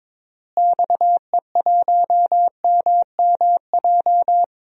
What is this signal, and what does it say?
Morse code, heard as a keyed tone, about 22 wpm, 700 Hz. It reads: XE1MMJ
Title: DXCC-100 Call Signs - 6 Characters - Repeated 2x - 22wpm